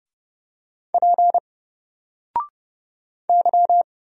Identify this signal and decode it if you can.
Morse code, heard as a keyed tone, about 30 wpm, 700 Hz. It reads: P Y
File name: Single-Letter-Number-30wpm